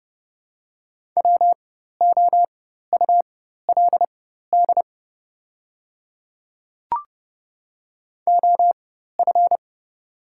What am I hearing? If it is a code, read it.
Morse code, heard as a keyed tone, about 32 wpm, spaced as for 14 wpm, 700 Hz. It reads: WOULD OF
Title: 4x Character Spacing - Top 100 Words - 30wpm